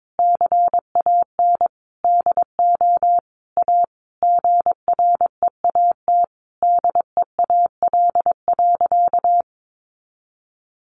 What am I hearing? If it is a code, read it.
Morse code, heard as a keyed tone, about 22 wpm, 700 Hz. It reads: CAN DO A GREAT DEAL.
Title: Sentences from Top 1300 Words 22wpm